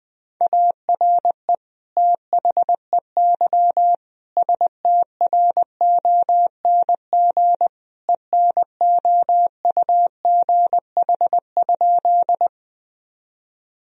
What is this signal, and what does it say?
Morse code, heard as a keyed tone, about 21 wpm, 700 Hz. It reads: ARE THEY STRONG ENOUGH?
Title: Sentences from Top 1000 Words - Review Part 3 - 20wpm